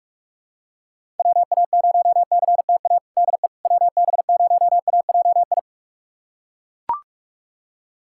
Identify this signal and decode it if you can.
Morse code, heard as a keyed tone, about 41 wpm, 700 Hz. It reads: WA0CTA DE WB0AJI
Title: USA Callsign DE Callsign - 6 Characters - 45wpm